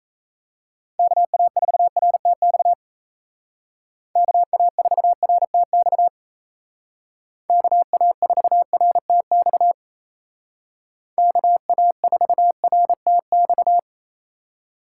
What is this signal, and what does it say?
Morse code, heard as a keyed tone, about 33 wpm, 700 Hz. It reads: KA4RTX KA4RTX KA4RTX KA4RTX
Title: Speed Racing - USA 2x3 Call Signs 28wpm